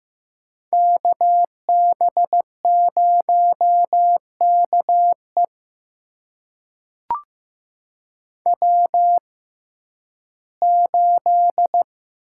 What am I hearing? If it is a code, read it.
Morse code, heard as a keyed tone, about 15 wpm, spaced as for 11 wpm, 700 Hz. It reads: KB0KE W8
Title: USA Call Signs - 5 Characters - 6x Character Spacing - 15wpm